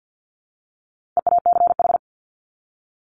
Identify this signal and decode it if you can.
Morse code, heard as a keyed tone, about 48 wpm, 700 Hz. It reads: EACH
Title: Top 200 Words 50wpm